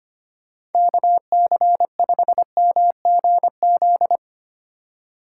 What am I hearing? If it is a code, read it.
Morse code, heard as a keyed tone, about 32 wpm, 700 Hz. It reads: KC5MGZ